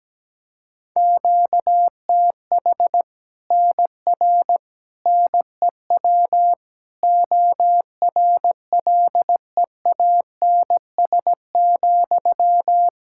Morse code, heard as a keyed tone, about 17 wpm, 700 Hz. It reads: QTH NR NEW ORLEANS,
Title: Practice QSO - Exchange 1 - Beginning - 17wpm